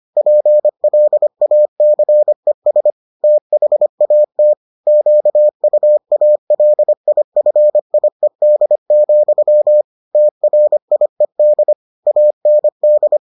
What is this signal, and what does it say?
Morse code, heard as a keyed tone, about 25 wpm, 600 Hz. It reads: PLACES THAT QUALIFIED, TRIED AND